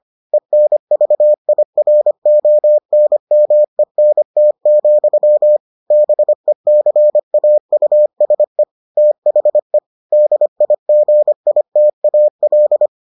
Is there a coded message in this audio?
Yes: Morse code from 0.3 to 12.9 s, heard as a keyed tone, about 25 wpm, 600 Hz. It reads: ENVIRONMENT, BECAUSE THE DIGITAL